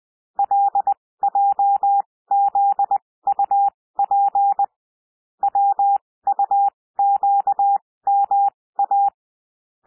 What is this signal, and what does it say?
Morse code, heard as a keyed tone, about 20 wpm, spaced as for 16 wpm, 800 Hz. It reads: LJZUP WUQMA
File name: cwmp3 / L1 SOUND LETTERS 20 wpm